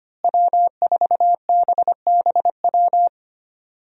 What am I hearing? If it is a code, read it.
Morse code, heard as a keyed tone, about 25 wpm, 700 Hz. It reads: W4BBW
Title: USA Call Signs - 5 Characters - Repeated 3x - 25wpm